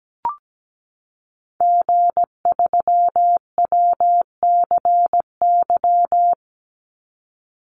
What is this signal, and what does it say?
Morse code, heard as a keyed tone, about 17 wpm, 700 Hz. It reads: G3WCY
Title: Call Signs Repeated 3x - 17wpm